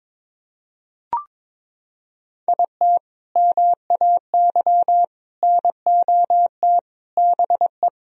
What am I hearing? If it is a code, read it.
Morse code, heard as a keyed tone, about 22 wpm, 700 Hz. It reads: IT MAY NOT BE
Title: Sentences from Top 1000 Words - Review Part 4 - 22wpm